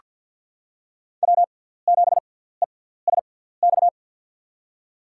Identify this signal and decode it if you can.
Morse code, heard as a keyed tone, about 47 wpm, 700 Hz. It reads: W 7 E S X
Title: USA Call Signs - 6x Character Spacing - 50wpm